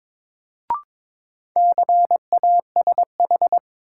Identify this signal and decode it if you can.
Morse code, heard as a keyed tone, about 22 wpm, 700 Hz. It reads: CASH
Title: Sentences from Top 700 Words - New Word Drill - 22wpm